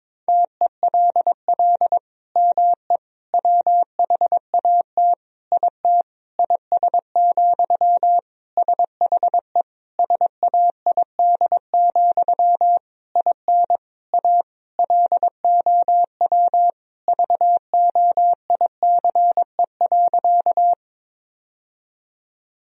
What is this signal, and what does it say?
Morse code, heard as a keyed tone, about 22 wpm, 700 Hz. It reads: TELL ME WHAT IT IS, SHE SAID, IN A LOW VOICE.
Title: Sentences from Top 1000 Words - Review Part 1 - 22wpm